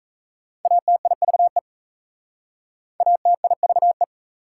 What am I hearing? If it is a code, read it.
Morse code, heard as a keyed tone, about 36 wpm, 700 Hz. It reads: ATIVE ATIVE